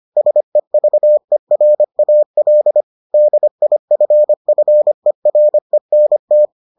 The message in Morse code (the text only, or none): SEVERAL DIFFERENT